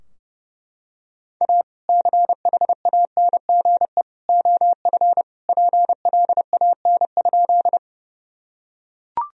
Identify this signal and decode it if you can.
Morse code, heard as a keyed tone, about 30 wpm, 700 Hz. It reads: A CHANGE OF PLAN?
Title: Sentences from Top 1000 Words - Review Part 2 - 30wpm